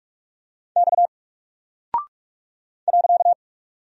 Morse code, heard as a keyed tone, about 44 wpm, 700 Hz. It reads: X .
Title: Single Character (Letter:Number:Punctuation:Prosign) Alt 45wpm